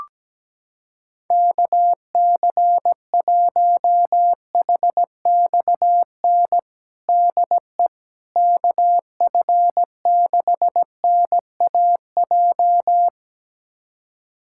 Morse code, heard as a keyed tone, about 17 wpm, 700 Hz. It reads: KC1HXN DE KF6NAJ
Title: USA Callsign DE Callsign - 6 Characters - 17wpm